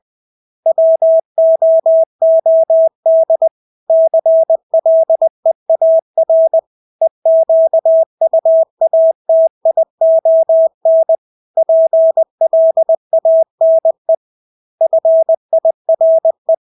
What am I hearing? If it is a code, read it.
Morse code, heard as a keyed tone, about 20 wpm, 650 Hz. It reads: WOOD CLEAR EQUATION PLANE FIRE
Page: Engelske ord 20 wpm